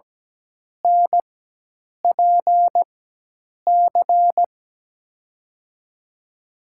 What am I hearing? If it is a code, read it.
Morse code, heard as a keyed tone, about 17 wpm, spaced as for 8 wpm, 700 Hz. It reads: NPC
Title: USA Call Signs - 6 Characters - 4x Character Spacing - 17wpm